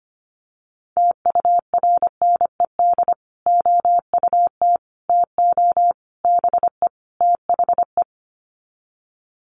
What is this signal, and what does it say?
Morse code, heard as a keyed tone, about 25 wpm, 700 Hz. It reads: TURNED OUT TO BE THE
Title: Sets of 5 Words Encore 25wpm